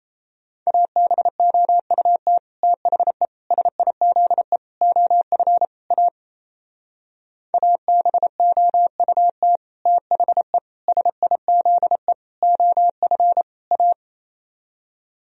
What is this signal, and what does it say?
Morse code, heard as a keyed tone, about 36 wpm, 700 Hz. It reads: ABOUT THE SIZE OF A ABOUT THE SIZE OF A